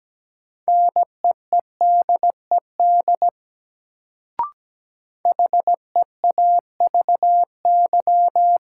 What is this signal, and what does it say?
Morse code, heard as a keyed tone, about 17 wpm, 700 Hz. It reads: NEEDED HEAVY
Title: Sentences from Top 1000 Words - Review Part 6 - New Word Drill - 17wpm